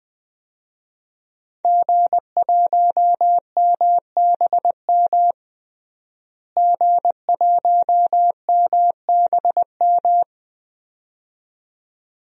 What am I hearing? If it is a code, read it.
Morse code, heard as a keyed tone, about 20 wpm, 700 Hz. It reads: G1MBM G1MBM
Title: DXCC-100 Call Signs - 5 Characters - Repeated 2x - 20wpm